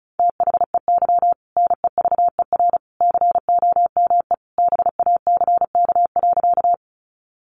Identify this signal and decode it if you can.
Morse code, heard as a keyed tone, about 34 wpm, 700 Hz. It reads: THEY NEVER COME BACK.
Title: Sentences from Top 300 Words 35wpm